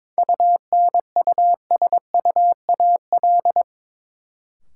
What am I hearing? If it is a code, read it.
Morse code, heard as a keyed tone, about 22 wpm, 700 Hz. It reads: UNUSUAL
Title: Sentences from Top 1400 Words - New Word Drill Alt - 22wpm